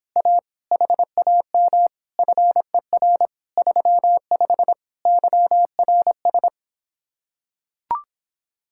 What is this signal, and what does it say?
Morse code, heard as a keyed tone, about 26 wpm, 700 Hz. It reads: A HAM FER 35 YRS